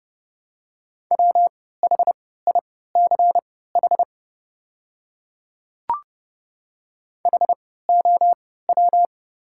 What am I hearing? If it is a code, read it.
Morse code, heard as a keyed tone, about 30 wpm, spaced as for 17 wpm, 700 Hz. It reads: WHICH HOW
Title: 3x Character Spacing - Top 200 Words - 30wpm